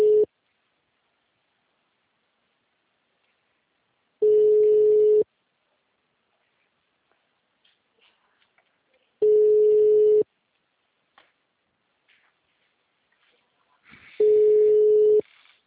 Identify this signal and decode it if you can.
Ringback tone (Europe, 425 Hz).